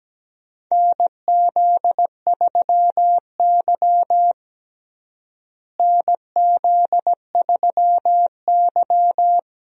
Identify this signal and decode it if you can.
Morse code, heard as a keyed tone, about 17 wpm, 700 Hz. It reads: NZ3Y NZ3Y
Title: USA Call Signs - 4 Characters - Repeated 2x - 17wpm